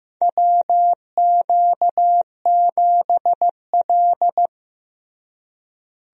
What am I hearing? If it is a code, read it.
Morse code, heard as a keyed tone, about 15 wpm, 700 Hz. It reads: WQ7L
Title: USA Call Signs - 4 Characters - Repeated 2x - 15wpm